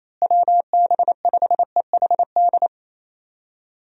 Morse code, heard as a keyed tone, about 28 wpm, 700 Hz. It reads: WB5EHD
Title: USA Call Signs - 6 Characters - Repeated 2x - 28wpm